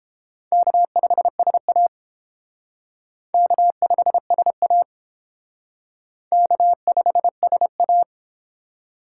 Morse code, heard as a keyed tone, about 29 wpm, 700 Hz. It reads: K5SA K5SA K5SA